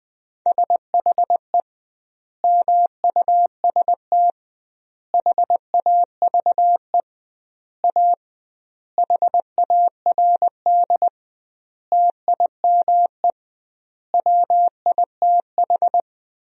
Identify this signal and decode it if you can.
Morse code, heard as a keyed tone, about 20 wpm, 700 Hz. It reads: SHE MUST HAVE A HARD TIME WITH
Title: 2x Word-Spacing - Sentences from Top 300 Words - 20wpm